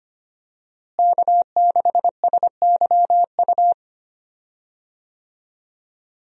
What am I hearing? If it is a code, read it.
Morse code, heard as a keyed tone, about 25 wpm, 700 Hz. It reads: K6SYU
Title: USA Call Signs - 5 Characters - Repeated 2x - 25wpm